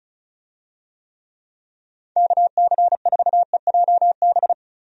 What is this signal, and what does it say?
Morse code, heard as a keyed tone, about 34 wpm, 700 Hz. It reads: KC4EJB
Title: USA Call Signs - 6 Characters - 35wpm